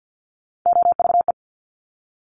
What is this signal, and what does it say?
Morse code, heard as a keyed tone, about 50 wpm, 690 Hz. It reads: OVE
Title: Top Three Letter Combinations in Words 50wpm